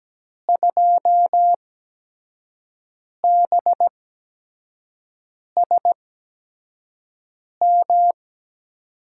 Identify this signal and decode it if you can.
Morse code, heard as a keyed tone, about 17 wpm, spaced as for 5 wpm, 700 Hz. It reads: 2BSM
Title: USA Call Signs - 6 Characters - 8x Character Spacing - 17wpm